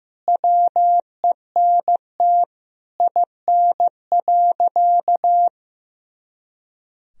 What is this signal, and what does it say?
Morse code, heard as a keyed tone, about 16 wpm, 700 Hz. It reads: WENT IN.
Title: Sentences from Top 300 Words and Abbreviations 15wpm